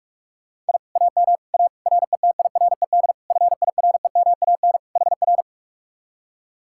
Morse code, heard as a keyed tone, about 64 wpm, spaced as for 54 wpm, 700 Hz. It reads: I AM A RETIRED FIREMAN HR
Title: Speed-Racing Common QSO Phrases - 45wpm